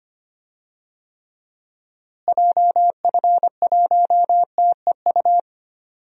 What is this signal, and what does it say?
Morse code, heard as a keyed tone, about 25 wpm, 700 Hz. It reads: JF1TEU
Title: DXCC-100 Call Signs - 6 Characters - Repeated 2x - 25wpm